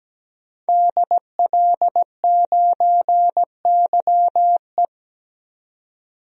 Morse code, heard as a keyed tone, about 17 wpm, 700 Hz. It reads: DL9YE
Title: DXCC-100 Call Signs - 5 Characters - 17wpm